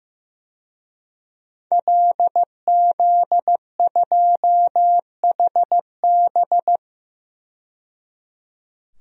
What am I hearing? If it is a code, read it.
Morse code, heard as a keyed tone, about 16 wpm, 700 Hz. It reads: LZ2HB
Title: DXCC-100 Call Signs - 5 Characters - 15wpm